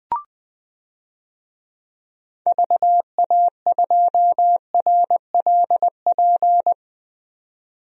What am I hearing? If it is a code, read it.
Morse code, heard as a keyed tone, about 20 wpm, 700 Hz. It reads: VA2RLP
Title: DXCC-100 Call Signs - 6 Characters - 20wpm